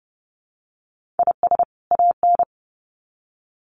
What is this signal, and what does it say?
Morse code, heard as a keyed tone, about 30 wpm, 700 Hz. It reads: IS AN
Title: Sets of 2 Words 30wpm